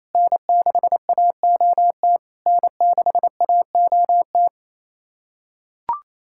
Morse code, heard as a keyed tone, about 28 wpm, 700 Hz. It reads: N6AOT N6AOT
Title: SOTA QSO Practice 28wpm